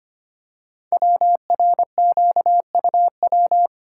Morse code, heard as a keyed tone, about 25 wpm, 700 Hz. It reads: WRQUW
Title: Code Groups - 5 Letters - 25wpm